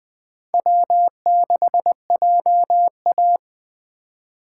Morse code, heard as a keyed tone, about 20 wpm, 700 Hz. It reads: W6JA